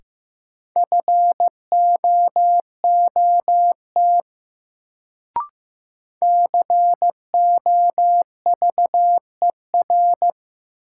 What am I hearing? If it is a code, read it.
Morse code, heard as a keyed tone, about 15 wpm, 700 Hz. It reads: FOOT COVER
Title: Sentences from Top 500 Words - New Word Drill - 15wpm